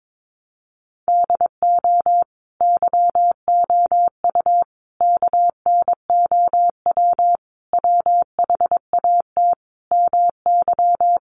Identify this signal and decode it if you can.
Morse code, heard as a keyed tone, about 22 wpm, 700 Hz. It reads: DO YOU KNOW WHAT MY
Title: Sentences from Top 1200 Words 22wpm